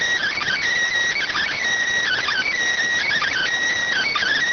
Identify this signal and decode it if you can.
Emergency Alert System (SAME) tones, heard as harsh buzzing bursts. It is heard from 0.2 to 4.3 s.